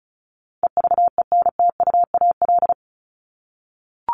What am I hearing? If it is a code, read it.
Morse code, heard as a keyed tone, about 34 wpm, 700 Hz. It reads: EVENTUAL
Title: LY Suffix - Top 100 Words with Base Words 35wpm